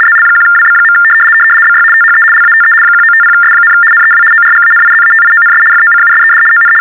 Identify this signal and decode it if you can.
RTTY, about 75 baud. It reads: ZL1WB. BY THE LATE 1950S, NEW ORGANIZATIONS FOCUSED ON
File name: teletype-RTTY-75